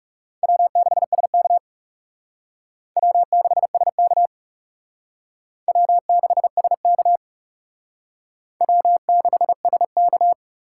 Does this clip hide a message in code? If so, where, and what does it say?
Yes: Morse code from 0.4 to 10.3 s, heard as a keyed tone, about 38 wpm, 700 Hz. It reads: W6SK W6SK W6SK W6SK